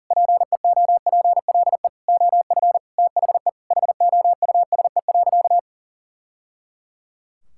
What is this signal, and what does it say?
Morse code, heard as a keyed tone, about 37 wpm, 700 Hz. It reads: PEOPLE OF THE HOUSE.
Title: Sentences from Top 1000 Words - Review Part 5 - 40wpm